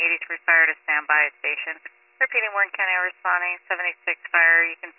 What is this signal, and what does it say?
Emergency Alert System (SAME) tones, heard as harsh buzzing bursts, with speech. It is heard from 0.9 to 4.7 s.